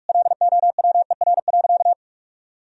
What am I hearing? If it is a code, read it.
Morse code, heard as a keyed tone, about 46 wpm, 700 Hz. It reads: LOWER.
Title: Sentences from Top 1000 Words - Review Part 4 - 45wpm